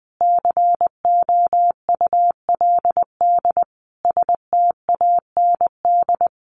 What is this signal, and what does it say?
Morse code, heard as a keyed tone, about 20 wpm, 700 Hz. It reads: COULD STAND
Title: Sentences from Top 1300 Words 20wpm